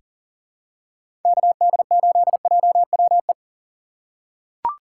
Morse code, heard as a keyed tone, about 39 wpm, 700 Hz. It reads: KD8JWE
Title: USA Call Signs - 4x Character Spacing - 40wpm